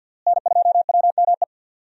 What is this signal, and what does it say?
Morse code, heard as a keyed tone, about 52 wpm, 700 Hz. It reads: N2WME
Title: USA Call Signs - 5 Characters - Repeated 2x - 50wpm